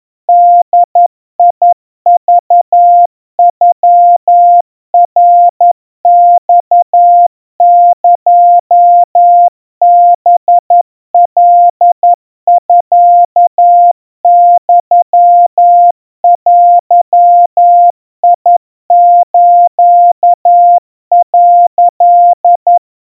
Morse code (Japanese Wabun code), heard as a keyed tone, about 11 wpm, 700 Hz. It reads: ボクノナマエハカミユデス」
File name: kawa / Morse sample / kamyu 40 cpm 700 Hz Crispy